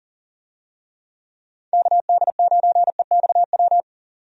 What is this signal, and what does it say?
Morse code, heard as a keyed tone, about 39 wpm, 700 Hz. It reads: KD9EXW
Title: USA Call Signs - 6 Characters - 40wpm